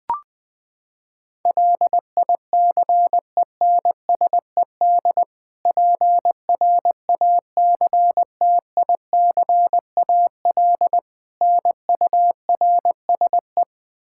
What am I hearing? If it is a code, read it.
Morse code, heard as a keyed tone, about 21 wpm, 700 Hz. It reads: LICENSED PRACTICAL NURSE
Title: Practice QSO - Exchange 3 - More Info - 20wpm